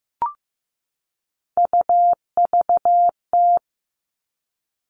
Morse code, heard as a keyed tone, about 15 wpm, 700 Hz. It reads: UVT
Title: All Cut Numbers - 3-Digit 15wpm